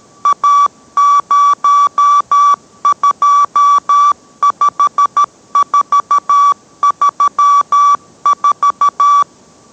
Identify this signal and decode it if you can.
Morse code, heard as a keyed tone, about 13 wpm, 1150 Hz. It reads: A025434